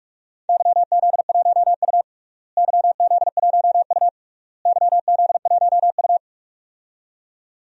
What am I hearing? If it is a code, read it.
Morse code, heard as a keyed tone, about 46 wpm, 700 Hz. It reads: YZ1U YZ1U YZ1U
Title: DXCC-100 Call Signs - 4 Characters - Repeated 3x - 45wpm